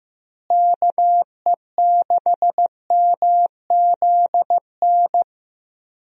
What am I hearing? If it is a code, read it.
Morse code, heard as a keyed tone, about 15 wpm, 700 Hz. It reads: KE6MZN
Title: USA Call Signs - 6 Characters - Repeated 3x - 15wpm